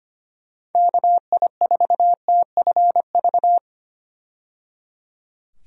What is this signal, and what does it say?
Morse code, heard as a keyed tone, about 25 wpm, 700 Hz. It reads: KI4TFV
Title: USA Call Signs - 6 Characters - 25wpm